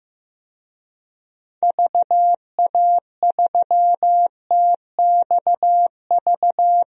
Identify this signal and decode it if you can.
Morse code, heard as a keyed tone, about 15 wpm, 700 Hz. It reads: VA3TXV
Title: DXCC-100 Call Signs - 6 Characters - Repeated 2x - 15wpm